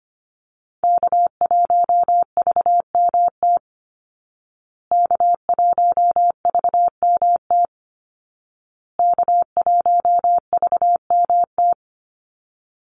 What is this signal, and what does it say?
Morse code, heard as a keyed tone, about 25 wpm, 700 Hz. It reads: K1VMT K1VMT K1VMT
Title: Call Signs Repeated 2x - 25wpm